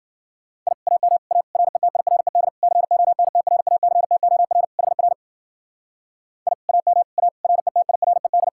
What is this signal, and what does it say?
Morse code, heard as a keyed tone, about 48 wpm, 700 Hz. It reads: I AM A RETIRED CONTRACTOR HR I AM A RETIRED
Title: Speed-Racing Common QSO Phrases Encore - 40wpm